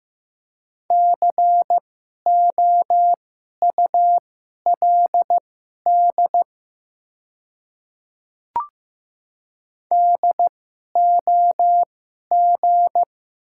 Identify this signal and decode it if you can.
Morse code, heard as a keyed tone, about 15 wpm, spaced as for 11 wpm, 700 Hz. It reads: COULD DOG
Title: 2x Character Spacing - Top 500 Words - 15wpm